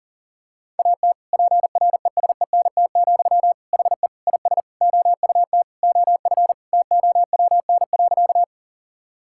Morse code, heard as a keyed tone, about 40 wpm, 700 Hz. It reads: AT PRESENT, HE IS OUT OF TOWN.
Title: Sentences from Top 1000 Words - Review Part 5 - 40wpm